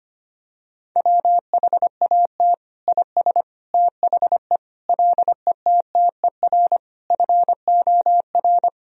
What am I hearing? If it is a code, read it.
Morse code, heard as a keyed tone, about 25 wpm, 700 Hz. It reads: WHAT IS THE LETTER FOR
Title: Sentences from Top 1000 Words - Review Part 2 - 25wpm